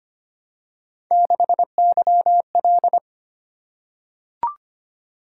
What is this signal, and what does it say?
Morse code, heard as a keyed tone, about 25 wpm, 700 Hz. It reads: 6YL
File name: Three Letter-Number Combinations - 25wpm